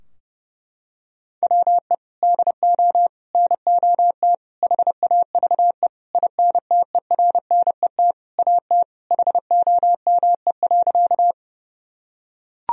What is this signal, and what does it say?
Morse code, heard as a keyed tone, about 32 wpm, 700 Hz. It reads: WE DO NOT HAVE INTERNET AT HOME.
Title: Sentences from Top 1000 Words - Review Part 4 - 30wpm